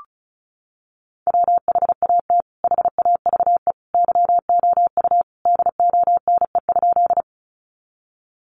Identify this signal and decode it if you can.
Morse code, heard as a keyed tone, about 34 wpm, 700 Hz. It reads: WHAT HAVE YOU DONE?
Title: Sentences from Top 500 Words 35wpm